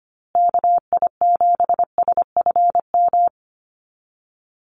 Morse code, heard as a keyed tone, about 25 wpm, 700 Hz. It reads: KI7SFM
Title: USA Call Signs Repeated 3x 25wpm